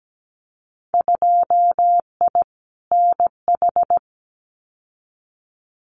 Morse code, heard as a keyed tone, about 17 wpm, 700 Hz. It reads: 2I NH
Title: Winter Field Day Exchanges Encore 17wpm